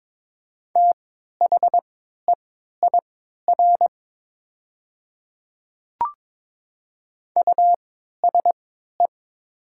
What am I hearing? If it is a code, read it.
Morse code, heard as a keyed tone, about 22 wpm, spaced as for 13 wpm, 700 Hz. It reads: THEIR USE
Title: 3x Character Spacing - Top 200 Words - 22wpm